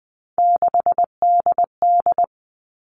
Morse code, heard as a keyed tone, about 20 wpm, 700 Hz. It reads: 6DD
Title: All Cut Numbers - 3-Digit 20wpm